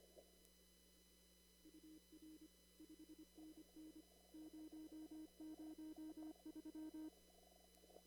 Morse code, heard as a keyed tone, about 25 wpm, 310 Hz. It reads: UR 5NN 003
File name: CW / kt5j wpx cw-op-k5pi